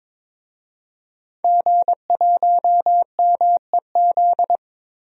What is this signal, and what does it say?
Morse code, heard as a keyed tone, about 22 wpm, 700 Hz. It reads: G1MEZ
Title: DXCC-100 Call Signs - 5 Characters - Repeated 2x - 22wpm